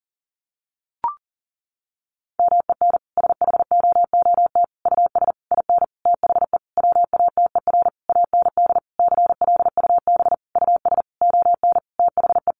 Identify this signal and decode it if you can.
Morse code, heard as a keyed tone, about 40 wpm, 700 Hz. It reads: MEN SHOOT US IN THE WATER AND CLUB US ON THE
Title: Sentences from Top 1200 Words 40wpm